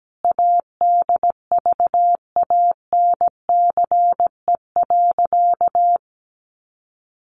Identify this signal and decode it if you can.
Morse code, heard as a keyed tone, about 17 wpm, 700 Hz. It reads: ADVANCE.
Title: Sentences from Top 1400 Words 17wpm